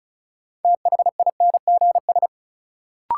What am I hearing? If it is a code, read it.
Morse code, heard as a keyed tone, about 36 wpm, 700 Hz. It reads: THINGS
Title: Sentences from Top 1000 Words - Review Part 4 - New Word Drill - 35wpm